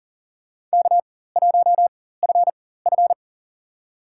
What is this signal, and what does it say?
Morse code, heard as a keyed tone, about 40 wpm, spaced as for 19 wpm, 700 Hz. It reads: K1FF
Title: USA Call Signs - 4 Characters - 4x Character Spacing - 40wpm